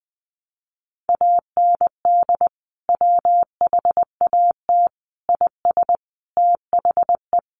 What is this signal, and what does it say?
Morse code, heard as a keyed tone, about 20 wpm, 700 Hz. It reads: AND WHAT IS THE
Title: Sentences from Top 800 Words 20wpm